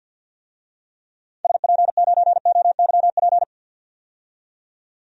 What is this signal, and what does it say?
Morse code, heard as a keyed tone, about 51 wpm, 700 Hz. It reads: SP9OYP
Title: DXCC-100 Call Signs - 6 Characters - 50wpm